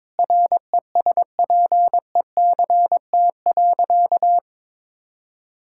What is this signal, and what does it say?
Morse code, heard as a keyed tone, about 22 wpm, 700 Hz. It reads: RESPECT.
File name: practice / 22wpm / Sentences with 4-Word Phrases - 22wpm